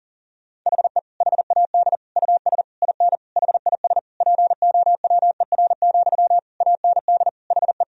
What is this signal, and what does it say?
Morse code, heard as a keyed tone, about 40 wpm, 700 Hz. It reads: HE HAD US IN HIS POWER, AND HE
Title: Sentences from Top 1000 Words - Review Part 2 - 40wpm